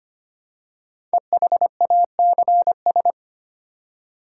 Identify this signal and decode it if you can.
Morse code, heard as a keyed tone, about 25 wpm, 700 Hz. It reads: EHACS
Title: Code Groups - 5 Letters - 25wpm